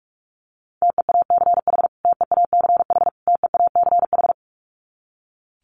Morse code, heard as a keyed tone, about 44 wpm, 700 Hz. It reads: TEACH TEACH TEACH
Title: Top 500 Words Repeated 45wpm